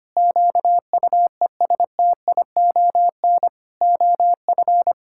Morse code, heard as a keyed tone, about 25 wpm, 700 Hz. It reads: QUESTION OF
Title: Sentences from Top 1600 Words 25wpm